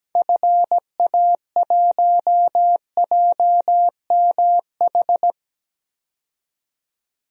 Morse code, heard as a keyed tone, about 17 wpm, 700 Hz. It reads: FA1JMH
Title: DXCC-100 Call Signs - 6 Characters - 17wpm